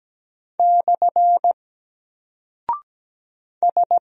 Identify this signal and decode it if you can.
Morse code, heard as a keyed tone, about 17 wpm, 700 Hz. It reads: / S
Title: Single Character (Letter:Number:Punctuation:Prosign) Alt 17wpm